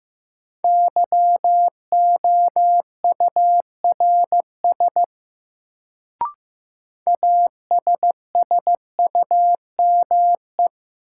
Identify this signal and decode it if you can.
Morse code, heard as a keyed tone, about 15 wpm, 700 Hz. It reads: YOURS ASSUME
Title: Sentences from Top 1500 Words - New Word Drill Alt - 15wpm